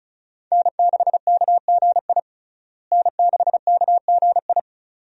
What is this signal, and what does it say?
Morse code, heard as a keyed tone, about 34 wpm, 700 Hz. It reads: N6KGI N6KGI
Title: USA Call Signs - 5 Characters - Repeated 2x - 35wpm